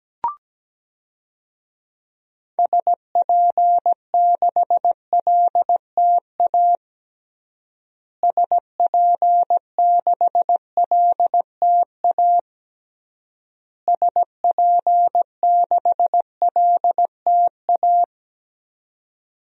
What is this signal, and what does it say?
Morse code, heard as a keyed tone, about 17 wpm, 700 Hz. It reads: SP6LTA SP6LTA SP6LTA
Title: DXCC-100 Call Signs - 6 Characters - Repeated 3x - 17wpm